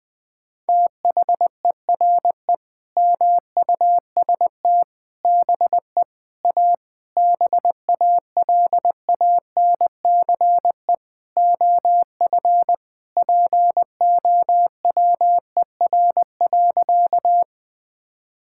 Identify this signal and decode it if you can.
Morse code, heard as a keyed tone, about 20 wpm, 700 Hz. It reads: THERE MUST BE A BALANCE OF POWER.
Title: Sentences from Top 1000 Words - Review Part 6 - 20wpm